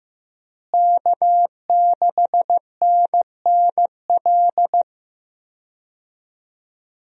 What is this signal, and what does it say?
Morse code, heard as a keyed tone, about 15 wpm, 700 Hz. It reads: K6NNL
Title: USA Call Signs - 5 Characters - Repeated 3x - 15wpm